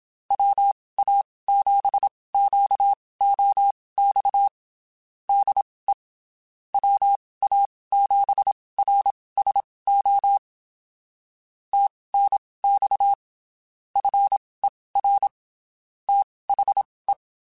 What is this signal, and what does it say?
Morse code, heard as a keyed tone, about 27 wpm, spaced as for 19 wpm, 800 Hz. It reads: WA7QOX DE WA7RSO TNX FER THE
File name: (spaced) / QSOs Spaced-mp3s / WA7QOX 2 @25 sp